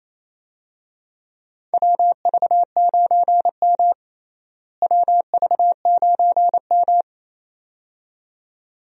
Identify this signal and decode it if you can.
Morse code, heard as a keyed tone, about 28 wpm, 700 Hz. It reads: WV9M WV9M
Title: USA Call Signs - 4 Characters - Repeated 2x - 28wpm